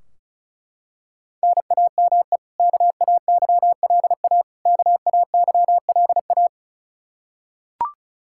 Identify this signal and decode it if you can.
Morse code, heard as a keyed tone, about 34 wpm, 700 Hz. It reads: NAME KAYLA KAYLA
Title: Practice QSO - Exchange 1 - Beginning - 35wpm